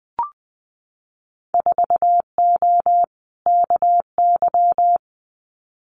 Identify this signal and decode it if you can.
Morse code, heard as a keyed tone, about 20 wpm, 700 Hz. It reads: 4O KY
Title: Winter Field Day Exchanges Encore 20wpm